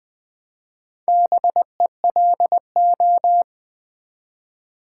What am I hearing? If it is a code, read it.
Morse code, heard as a keyed tone, about 20 wpm, 700 Hz. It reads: BELO
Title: Build A Word - 1 Letter At A Time - 3 to 5-Letter Words - 20wpm